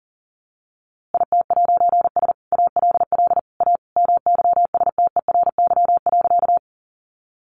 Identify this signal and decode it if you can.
Morse code, heard as a keyed tone, about 40 wpm, 700 Hz. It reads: IT'S ALL A MYSTERY.
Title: A Sound Of Thunder 40wpm